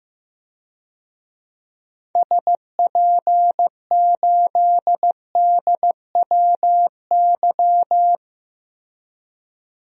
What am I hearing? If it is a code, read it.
Morse code, heard as a keyed tone, about 15 wpm, 700 Hz. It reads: SP8DWY
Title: DXCC-100 Call Signs - 6 Characters - 15wpm